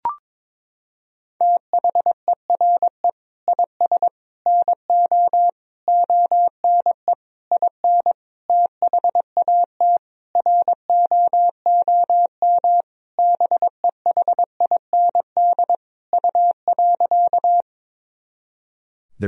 Morse code, heard as a keyed tone, about 22 wpm, 700 Hz. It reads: THERE IS NO ONE IN THAT ROOM BEHIND U.
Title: Sentences from Top 500 Words and Abbreviations 22wpm